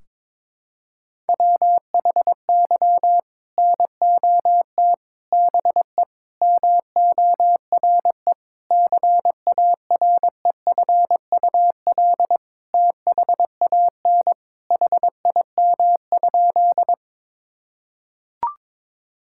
Morse code, heard as a keyed tone, about 21 wpm, 700 Hz. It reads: WHY NOT BE MORE CAREFUL THAN HIM?
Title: Sentences from Top 1600 Words 22wpm